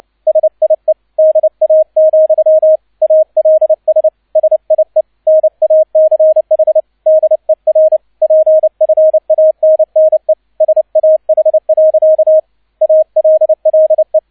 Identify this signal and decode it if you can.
Morse code, heard as a keyed tone, about 29 wpm, 630 Hz, with radio noise. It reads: SIE DA, ALS SIE NACH DER PFANNE SAH. ALLE
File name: 002 Zweiter Streich 30wpm